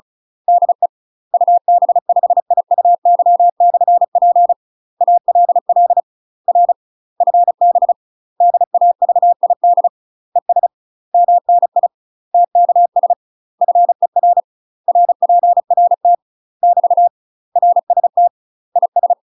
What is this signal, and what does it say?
Morse code, heard as a keyed tone, about 34 wpm, 700 Hz. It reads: DE UB5IUY/P ALL R FB DAVID ES MNI TKS FER RPRT <BT> RST IS
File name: UK MASTER QSOs TRAINING / 35WPM / UK MASTER QSOs 0005